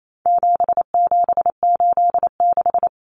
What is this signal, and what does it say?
Morse code, heard as a keyed tone, about 28 wpm, 700 Hz. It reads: 7786
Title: Numbers - 4-Digit 28wpm